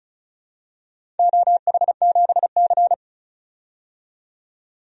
Morse code, heard as a keyed tone, about 36 wpm, 700 Hz. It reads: OH7C
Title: DXCC-100 Call Signs - 4 Characters - 35wpm